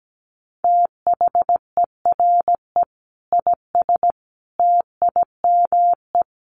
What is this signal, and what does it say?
Morse code, heard as a keyed tone, about 17 wpm, 700 Hz. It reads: THERE IS TIME
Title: Sentences from Top 1200 Words 17wpm